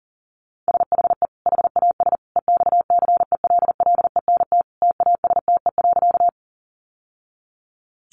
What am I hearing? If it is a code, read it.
Morse code, heard as a keyed tone, about 40 wpm, 700 Hz. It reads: SHE HAS EXCELLENT TASTE.
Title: Sentences from Top 1100 Words 40wpm